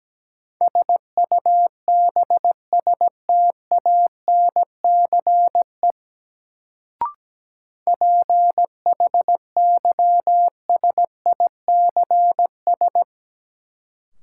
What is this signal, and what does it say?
Morse code, heard as a keyed tone, about 17 wpm, 700 Hz. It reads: SUBSTANCE PHYSICS
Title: Sentences from Top 1500 Words - New Word Drill Alt - 17wpm